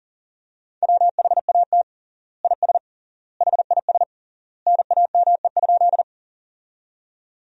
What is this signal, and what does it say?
Morse code, heard as a keyed tone, about 40 wpm, 700 Hz. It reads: WHAT IS HIS NAME?
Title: 3x Word-Spacing - Sentences from Top 200 Words - 40wpm